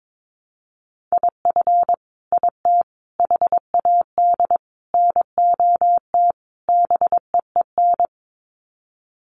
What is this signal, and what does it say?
Morse code, heard as a keyed tone, about 22 wpm, 700 Hz. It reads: IF IT HAD NOT BEEN
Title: Sets of 5 Words Encore 22wpm